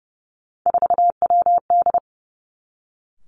Morse code, heard as a keyed tone, about 32 wpm, 700 Hz. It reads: 4WD
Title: Three Challenging Letter-Number Combinations 30wpm